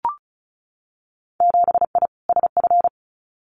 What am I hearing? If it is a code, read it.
Morse code, heard as a keyed tone, about 36 wpm, 700 Hz. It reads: 7I SF
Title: Winter Field Day Exchanges Encore 35wpm